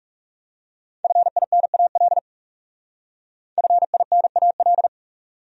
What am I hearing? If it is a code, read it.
Morse code, heard as a keyed tone, about 38 wpm, 700 Hz. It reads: FINAL FINAL